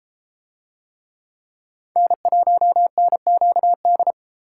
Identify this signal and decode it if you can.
Morse code, heard as a keyed tone, about 31 wpm, 700 Hz. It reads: N1NQD